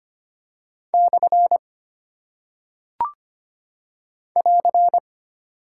Morse code, heard as a keyed tone, about 25 wpm, 700 Hz. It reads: / <AR>
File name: Single Punctuation-Prosign - 25wpm